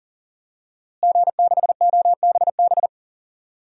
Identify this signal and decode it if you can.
Morse code, heard as a keyed tone, about 40 wpm, 700 Hz. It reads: G6OBB
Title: DXCC-100 Call Signs - 5 Characters - Repeated 2x - 40wpm